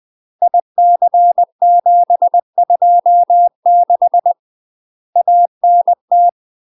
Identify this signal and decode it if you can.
Morse code, heard as a keyed tone, about 20 wpm, 700 Hz. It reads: IC726 ANT
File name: UK MASTER QSOs TRAINING / 20WPM / UK MASTER QSOs 0066